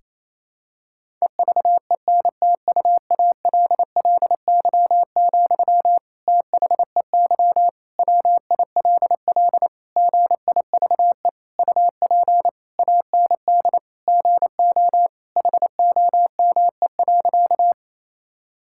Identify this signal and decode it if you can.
Morse code, heard as a keyed tone, about 28 wpm, 700 Hz. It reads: EVENTUALLY, THEY WILL GIVE UP AND GO HOME.
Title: Sentences from Top 1000 Words - Review Part 6 - 28wpm